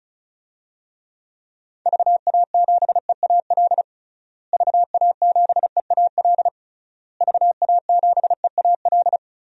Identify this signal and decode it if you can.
Morse code, heard as a keyed tone, about 34 wpm, 700 Hz. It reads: VA7EAL VA7EAL VA7EAL
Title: DXCC-100 Call Signs - 6 Characters - Repeated 3x - 35wpm